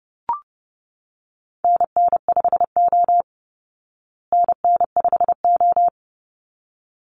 Morse code, heard as a keyed tone, about 30 wpm, 700 Hz. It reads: NN5O NN5O
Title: Call Signs Repeated 3x - 30wpm